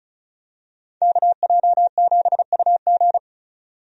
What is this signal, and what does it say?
Morse code, heard as a keyed tone, about 36 wpm, 700 Hz. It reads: KJ7UG
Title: USA Call Signs - 5 Characters - 35wpm